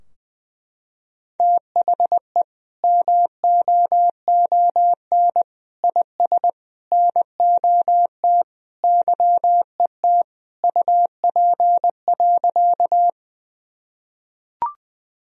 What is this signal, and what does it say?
Morse code, heard as a keyed tone, about 20 wpm, 700 Hz. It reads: THE MOON IS NOT YET UP.
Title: Sentences from Top 1000 Words - Review Part 3 - 20wpm